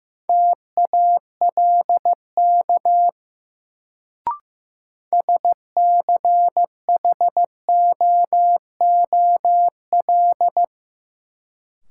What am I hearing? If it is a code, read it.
Morse code, heard as a keyed tone, about 15 wpm, 700 Hz. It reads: TALK SCHOOL
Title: Sets of 2 Words - New Word Drill - 15wpm